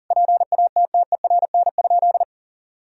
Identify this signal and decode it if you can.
Morse code, heard as a keyed tone, about 40 wpm, 700 Hz. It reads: PATTERN?
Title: Sentences from Top 1000 Words - Review Part 2 - 40wpm